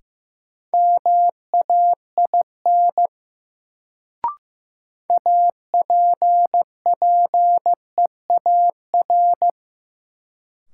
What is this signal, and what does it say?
Morse code, heard as a keyed tone, about 15 wpm, 700 Hz. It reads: MAIN APPEAR
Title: Sentences from Top 1000 Words - Review Part 2 - New Word Drill - 15wpm